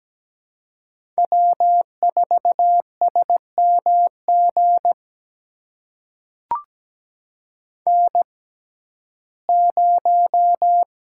Morse code, heard as a keyed tone, about 17 wpm, spaced as for 12 wpm, 700 Hz. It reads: W4SMG N0
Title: USA Call Signs - 5 Characters - 6x Character Spacing - 17wpm